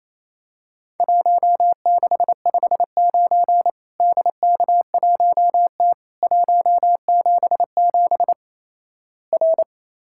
Morse code, heard as a keyed tone, about 28 wpm, 700 Hz. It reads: 1659 DK1T 177 R
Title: Worked All Europe DX Contest - QTC Exchange - 28wpm